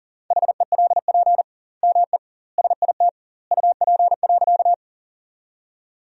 Morse code, heard as a keyed tone, about 40 wpm, 700 Hz. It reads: HELP ME SIT UP.
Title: 2x Word-Spacing - Sentences from Top 500 Words - 40wpm